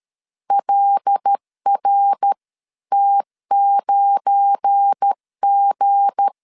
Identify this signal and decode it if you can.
Morse code, heard as a keyed tone, about 13 wpm, 800 Hz. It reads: LRT9G
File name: LEZIONI / 9 28 12wpm 2351